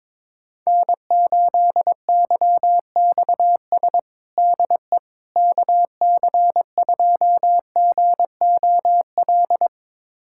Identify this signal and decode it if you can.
Morse code, heard as a keyed tone, about 21 wpm, 700 Hz. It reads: N8YXS DE KC2GOL
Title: USA Callsign DE Callsign 22wpm